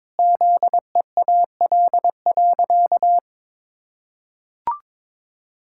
Morse code, heard as a keyed tone, about 22 wpm, 700 Hz. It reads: ZEAL.
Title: Pangram Challenge v2 - 22wpm